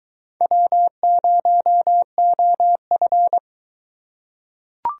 Morse code, heard as a keyed tone, about 23 wpm, 700 Hz. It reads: W0OF